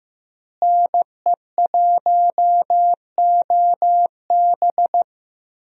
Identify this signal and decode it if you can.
Morse code, heard as a keyed tone, about 15 wpm, 700 Hz. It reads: NE1OB
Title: USA Call Signs - 5 Characters - Repeated 2x - 15wpm